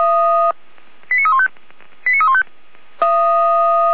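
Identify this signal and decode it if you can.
Five-tone selective call (ZVEI), heard as quick runs of five tones, tones of about 70 ms. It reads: 89315 89315
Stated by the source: ZVEI five-tone selective call 89310, received on an SDR.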